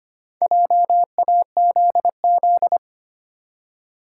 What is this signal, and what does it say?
Morse code, heard as a keyed tone, about 25 wpm, 700 Hz. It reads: JAZZ
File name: practice / 25wpm / Easy Words with Hints - 25wpm